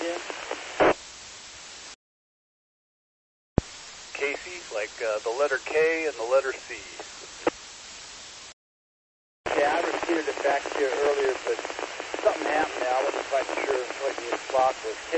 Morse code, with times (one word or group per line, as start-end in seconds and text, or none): none